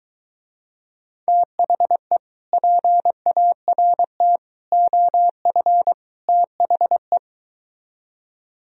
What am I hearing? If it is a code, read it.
Morse code, heard as a keyed tone, about 23 wpm, 700 Hz. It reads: THE PART OF THE